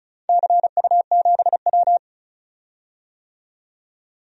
Morse code, heard as a keyed tone, about 36 wpm, 700 Hz. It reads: CU7W
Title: DXCC-100 Call Signs - 4 Characters - 35wpm